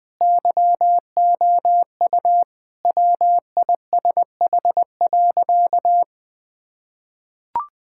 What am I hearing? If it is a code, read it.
Morse code, heard as a keyed tone, about 21 wpm, 700 Hz. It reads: YOU WISH.
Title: Sentences from Top 1600 Words 20wpm